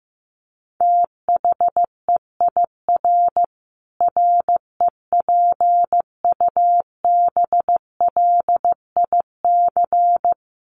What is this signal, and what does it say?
Morse code, heard as a keyed tone, about 14 wpm, 700 Hz. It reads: THEIR REPUBLIC
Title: Sentences from Top 1500 Words 15wpm